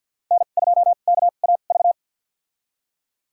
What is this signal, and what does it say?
Morse code, heard as a keyed tone, about 47 wpm, spaced as for 35 wpm, 700 Hz. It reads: N2KAV
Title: USA Call Signs - 5 Characters - 2x Character Spacing - 50wpm